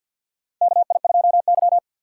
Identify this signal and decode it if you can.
Morse code, heard as a keyed tone, about 47 wpm, 700 Hz. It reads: KI2Y
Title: USA Call Signs - 4 Characters - 6x Character Spacing - 50wpm